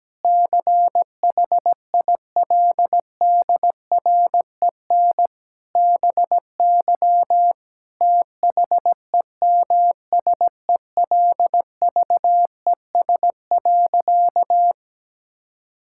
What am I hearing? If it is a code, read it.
Morse code, heard as a keyed tone, about 17 wpm, 700 Hz. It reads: CHILDREN BY THEMSELVES.
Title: Sentences from Top 1000 Words - Review Part 5 - 17wpm